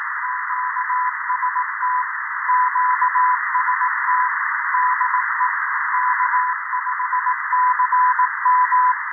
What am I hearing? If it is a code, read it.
Morse code, heard as a keyed tone, about 18 wpm, 1000 Hz, with noise. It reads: QST QST DE K5CM
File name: 0340z-fmt k5cm 80m by pa0pje